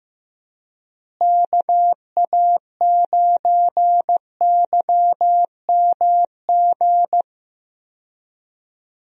Morse code, heard as a keyed tone, about 15 wpm, 700 Hz. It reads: KA9YMG
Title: USA Call Signs - 6 Characters - Repeated 3x - 15wpm